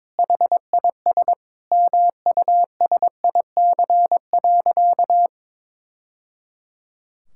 Morse code, heard as a keyed tone, about 22 wpm, 700 Hz. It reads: HIS MUSIC.
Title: Sentences from Top 1000 Words - Review Part 6 - 22wpm